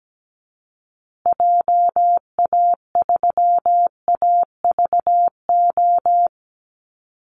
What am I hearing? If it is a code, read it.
Morse code, heard as a keyed tone, about 17 wpm, 700 Hz. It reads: JA3AVO
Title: Call Signs Repeated 3x - 17wpm